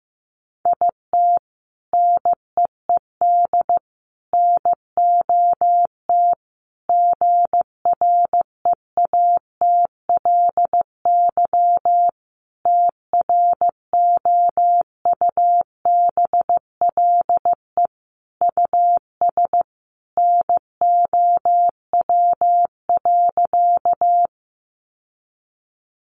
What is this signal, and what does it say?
Morse code, heard as a keyed tone, about 15 wpm, 700 Hz. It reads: IT NEED NOT GREATLY TROUBLE US NOW.
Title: Sentences from Top 1400 Words 15wpm